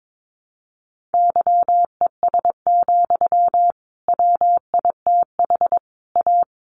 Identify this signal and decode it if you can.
Morse code, heard as a keyed tone, about 22 wpm, 700 Hz. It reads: YES, WITH A
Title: Sentences from Top 1300 Words 22wpm